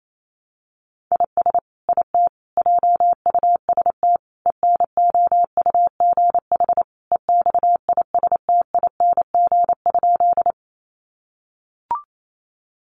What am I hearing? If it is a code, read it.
Morse code, heard as a keyed tone, about 28 wpm, 700 Hz. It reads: IS IT JUST ENOUGH EXISTING?
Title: Sentences from Top 1500 Words 28wpm